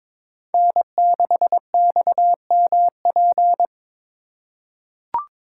Morse code, heard as a keyed tone, about 22 wpm, 700 Hz. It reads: N6XMP
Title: USA Call Signs - 4x Character Spacing - 22wpm